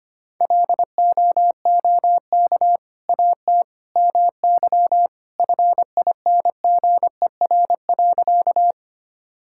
Morse code, heard as a keyed tone, about 25 wpm, 700 Hz. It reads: LOOK AT MY FINGER.
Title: Sentences from Top 1600 Words 25wpm